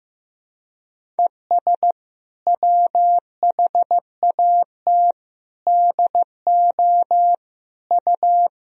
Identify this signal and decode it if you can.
Morse code, heard as a keyed tone, about 16 wpm, 700 Hz. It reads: ES WHAT DO U
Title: Sentences from Top 100 Words and Abbreviations 15wpm